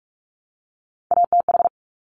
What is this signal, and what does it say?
Morse code, heard as a keyed tone, about 46 wpm, 700 Hz. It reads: ATH
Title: Top Three Letter Combinations in Words 45wpm